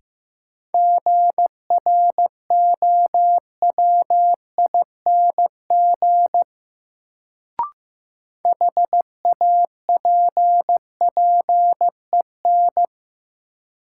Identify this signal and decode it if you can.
Morse code, heard as a keyed tone, about 15 wpm, 700 Hz. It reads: GROWING HAPPEN
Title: Sets of 5 Words Encore - New Word Drill - 15wpm